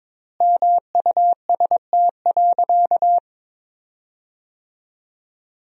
Morse code, heard as a keyed tone, about 22 wpm, 700 Hz. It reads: MUST.
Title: 4x Word-Spacing - Sentences from Top 200 Words - 22wpm